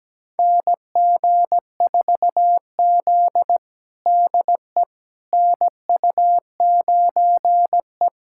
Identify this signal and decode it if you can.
Morse code, heard as a keyed tone, about 17 wpm, 700 Hz. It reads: NG4Z DE NU9E
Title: USA Callsign DE Callsign - 4 Characters - 17wpm